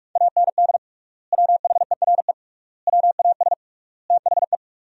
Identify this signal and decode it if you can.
Morse code, heard as a keyed tone, about 47 wpm, 700 Hz. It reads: AND WHERE WAS THE
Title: 3x Word-Spacing - Sentences from Top 200 Words - 45wpm